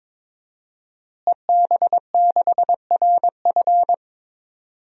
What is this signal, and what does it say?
Morse code, heard as a keyed tone, about 22 wpm, 700 Hz. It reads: EB6RF
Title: DXCC-100 Call Signs - 5 Characters - Repeated 3x - 22wpm